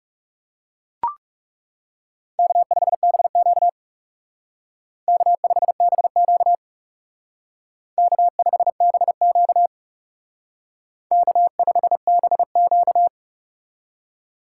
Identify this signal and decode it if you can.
Morse code, heard as a keyed tone, about 38 wpm, 700 Hz. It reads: K5BQ K5BQ K5BQ K5BQ